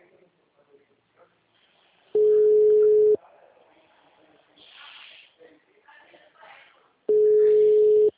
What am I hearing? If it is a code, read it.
Ringback tone (Europe, 425 Hz), with speech.